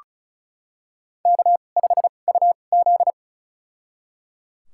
Morse code, heard as a keyed tone, about 36 wpm, spaced as for 25 wpm, 700 Hz. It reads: K5UZ
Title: USA Call Signs - 4 Characters - 2x Character Spacing - 35wpm